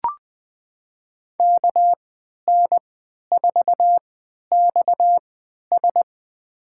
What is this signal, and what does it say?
Morse code, heard as a keyed tone, about 20 wpm, spaced as for 11 wpm, 700 Hz. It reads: KN4XS
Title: USA Call Signs - 5 Characters - 3x Character Spacing - 20wpm